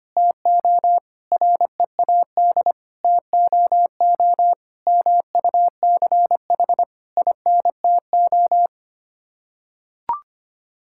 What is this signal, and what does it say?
Morse code, heard as a keyed tone, about 25 wpm, 700 Hz. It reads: TO READ TOO MUCH INTO
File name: Sets of 9 Words - 25wpm